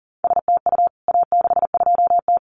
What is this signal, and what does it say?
Morse code, heard as a keyed tone, about 40 wpm, 700 Hz. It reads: STU A62T
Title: CWT Exchange 40wpm